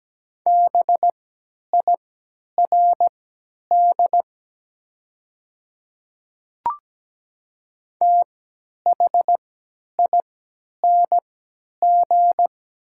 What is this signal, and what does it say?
Morse code, heard as a keyed tone, about 17 wpm, spaced as for 10 wpm, 690 Hz. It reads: BIRD THING
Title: 3x Character Spacing - Top 500 Words - 17wpm